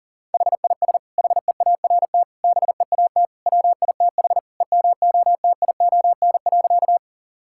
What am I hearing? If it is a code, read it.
Morse code, heard as a keyed tone, about 40 wpm, 700 Hz. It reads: HIS HEART BEAT WITH EMOTION.
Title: Sentences from Top 1600 Words 40wpm